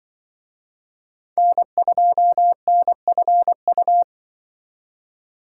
Morse code, heard as a keyed tone, about 25 wpm, 700 Hz. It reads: N2NFU
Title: Speed Racing - USA Call Signs 20wpm